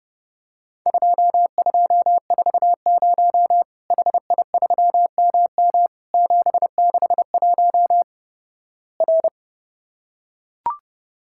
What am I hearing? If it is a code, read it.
Morse code, heard as a keyed tone, about 32 wpm, 700 Hz. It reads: 2240 HI3MM 761 R
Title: Worked All Europe DX Contest - QTC Exchange - 30wpm